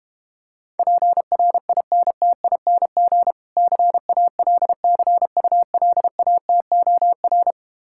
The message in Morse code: PRINTING CALCULATOR

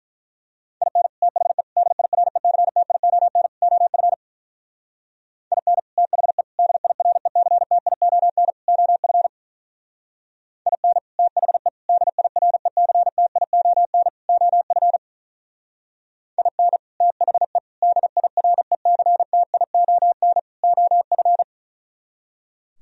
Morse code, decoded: IN THE DIRECTION OF IN THE DIRECTION OF IN THE DIRECTION OF IN THE DIRECTION OF

